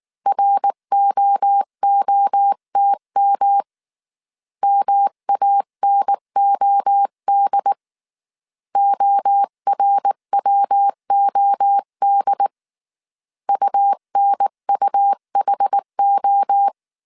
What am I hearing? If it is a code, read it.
Morse code, heard as a keyed tone, about 19 wpm, 800 Hz. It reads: ROOTM MANOD ORWOD UNUHO
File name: CW / DET / 1 3 1500C S B 18wpm 1916